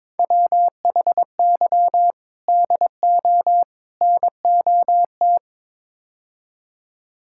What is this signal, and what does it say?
Morse code, heard as a keyed tone, about 22 wpm, 700 Hz. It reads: WHY DO NOT
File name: practice / 22wpm / Build A Phrase - 5-Word Phrases - 22wpm